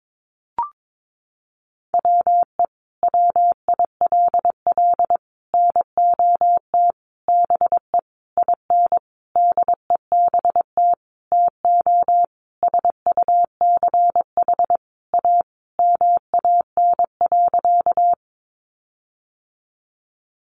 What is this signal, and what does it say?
Morse code, heard as a keyed tone, about 22 wpm, 700 Hz. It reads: WE WILL NOT BE IN DEBT TO SUCH A MAN.
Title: Sentences from Top 1300 Words 22wpm